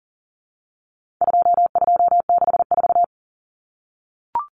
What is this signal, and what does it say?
Morse code, heard as a keyed tone, about 40 wpm, 700 Hz. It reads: 2264
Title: Numbers - 4-Digit 40wpm